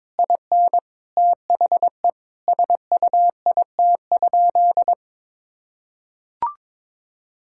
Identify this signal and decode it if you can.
Morse code, heard as a keyed tone, about 22 wpm, 700 Hz. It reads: IN THE SUIT?
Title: Sentences from Top 1600 Words 22wpm